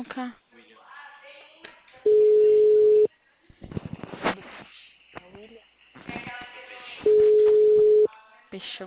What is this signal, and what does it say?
Ringback tone (Europe, 425 Hz), with speech.
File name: out- -2003- - - .73121